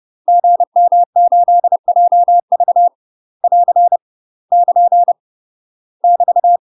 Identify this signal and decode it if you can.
Morse code, heard as a keyed tone, about 30 wpm, 700 Hz. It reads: GM8JV <AR> <KN> <BT>
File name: UK MASTER QSOs TRAINING/30WPM